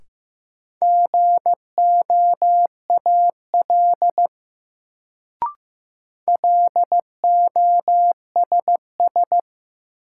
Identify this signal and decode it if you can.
Morse code, heard as a keyed tone, about 15 wpm, 700 Hz. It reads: GOAL LOSS
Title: Sentences from Top 1400 Words - New Word Drill Alt - 15wpm